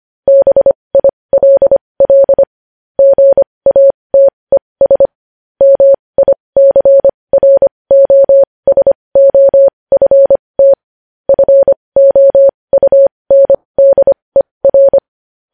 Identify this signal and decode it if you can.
Morse code, heard as a keyed tone, about 25 wpm, spaced as for 20 wpm, 550 Hz. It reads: BILL GATES MICROSOFT FOUNDER
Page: news-headlines-in-morse-code-at-20-wpm